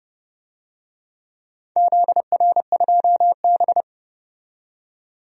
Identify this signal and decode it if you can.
Morse code, heard as a keyed tone, about 29 wpm, 700 Hz. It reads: ZR2B
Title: DXCC-100 Call Signs - 4 Characters - 30wpm